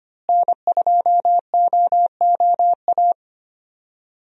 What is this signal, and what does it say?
Morse code, heard as a keyed tone, about 25 wpm, 700 Hz. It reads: N2OOA
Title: USA Call Signs - 5 Characters - Repeated 2x - 25wpm